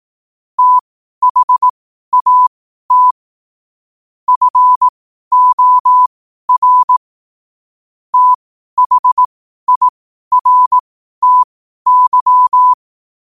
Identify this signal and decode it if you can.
Morse code, heard as a keyed tone, about 18 wpm, spaced as for 12 wpm, 1000 Hz. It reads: THAT FOR THIRTY
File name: qotdfiles / 12wpm